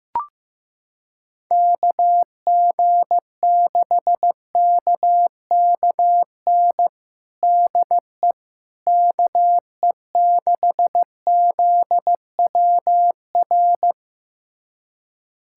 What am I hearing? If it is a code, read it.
Morse code, heard as a keyed tone, about 15 wpm, 700 Hz. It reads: KG6KKN DE KE6ZWR
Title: USA Callsign DE Callsign - 6 Characters - 15wpm